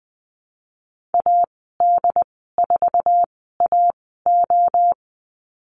Morse code, heard as a keyed tone, about 20 wpm, spaced as for 15 wpm, 700 Hz. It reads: AD4AO
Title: 076 Lesson 13g - Letter L - Call Signs 20wpm